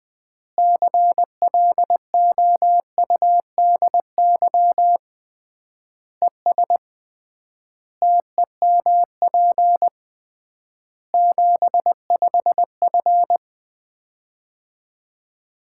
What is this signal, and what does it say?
Morse code, heard as a keyed tone, about 20 wpm, 700 Hz. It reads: CLOUDY ES TEMP 75F
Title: Common QSO Phrases - 3x Word-Spacing - 20wpm